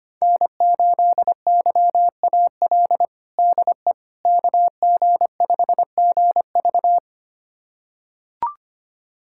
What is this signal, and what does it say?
Morse code, heard as a keyed tone, about 25 wpm, 700 Hz. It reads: N8YAL DE KG5GV
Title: USA Callsign DE Callsign - 5 Characters - 25wpm